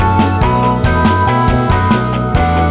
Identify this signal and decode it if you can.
There is no code.